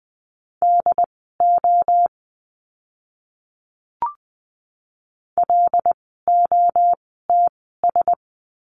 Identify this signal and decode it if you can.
Morse code, heard as a keyed tone, about 20 wpm, spaced as for 14 wpm, 700 Hz. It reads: DO LOTS
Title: 075 Lesson 13f - Letter L - Words 20wpm